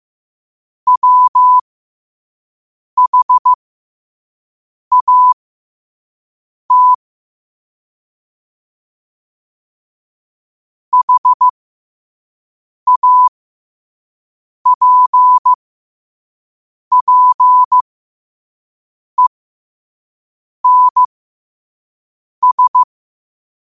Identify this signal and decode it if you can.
Morse code, heard as a keyed tone, about 15 wpm, spaced as for 5 wpm, 1000 Hz. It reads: WHAT HAPPENS
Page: New quotes every day in morse code at 5 Words per minute.